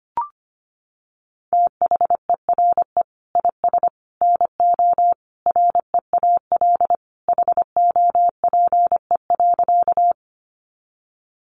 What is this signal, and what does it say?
Morse code, heard as a keyed tone, about 25 wpm, 700 Hz. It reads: THERE IS NO REAL HOPE.
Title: Sentences from Top 1000 Words 25wpm